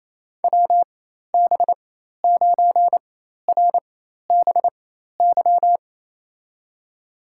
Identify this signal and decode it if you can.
Morse code, heard as a keyed tone, about 29 wpm, spaced as for 13 wpm, 700 Hz. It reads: WB9RBY
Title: USA Call Signs - 6 Characters - 4x Character Spacing - 28wpm